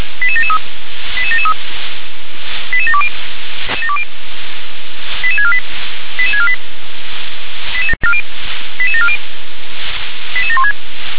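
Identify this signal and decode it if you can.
Five-tone selective call (ZVEI), heard as quick runs of five tones, tones of about 70 ms. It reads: 88773 88773 88720 88648 88648 88740 88716
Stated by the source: ZVEI five-tone call received on SDR.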